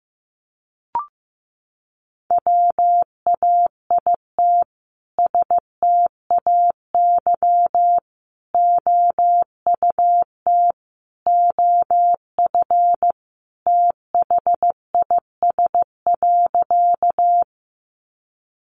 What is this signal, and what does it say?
Morse code, heard as a keyed tone, about 15 wpm, 700 Hz. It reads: WAIT STAY OUT OF THIS.
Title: A Sound Of Thunder 15wpm